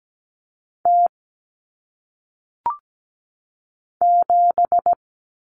Morse code, heard as a keyed tone, about 17 wpm, 700 Hz. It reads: T 7
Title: Common Cut Numbers - 1-Digit 17wpm